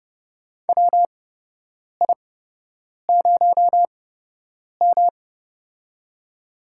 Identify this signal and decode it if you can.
Morse code, heard as a keyed tone, about 30 wpm, spaced as for 8 wpm, 700 Hz. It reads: WI0M
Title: USA Call Signs - 8x Character Spacing - 30wpm